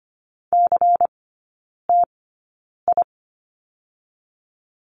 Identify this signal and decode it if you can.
Morse code, heard as a keyed tone, about 25 wpm, spaced as for 9 wpm, 700 Hz. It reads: CTI
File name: course / 25:10wpm / 093 Lesson 16e - Letter c - Three Characters 25wpm